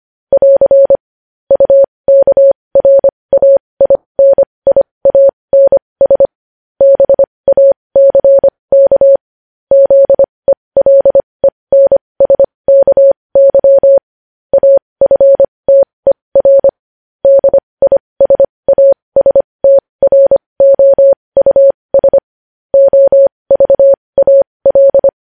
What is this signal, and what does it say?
Morse code, heard as a keyed tone, about 25 wpm, spaced as for 20 wpm, 550 Hz. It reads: <AR> UKRAINIANS BACK ZELENSKY AFTER DISASTROUS OVAL
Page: news-headlines-in-morse-code-at-20-wpm